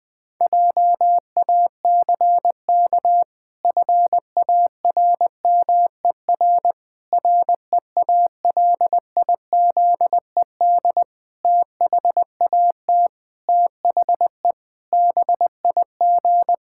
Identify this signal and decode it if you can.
Morse code, heard as a keyed tone, about 20 wpm, 700 Hz. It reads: JACK FARMER REALIZED THAT THE BIG
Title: Pangram Challenge v2 - 20wpm